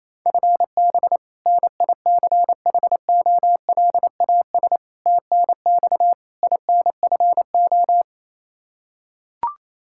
Morse code, heard as a keyed tone, about 28 wpm, 700 Hz. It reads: FB NICHOLAS TNX INFO
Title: K1USN SST QSO Practice 28wpm